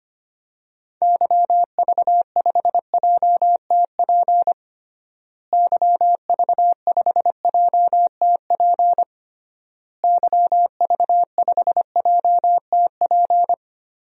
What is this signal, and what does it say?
Morse code, heard as a keyed tone, about 25 wpm, 700 Hz. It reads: YV5JTP YV5JTP YV5JTP
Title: DXCC-100 Call Signs - 6 Characters - Repeated 3x - 25wpm